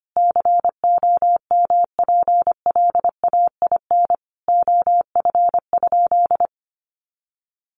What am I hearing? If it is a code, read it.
Morse code, heard as a keyed tone, about 25 wpm, 700 Hz. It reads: COMPLAIN OF?
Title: Sentences from Top 1300 Words 25wpm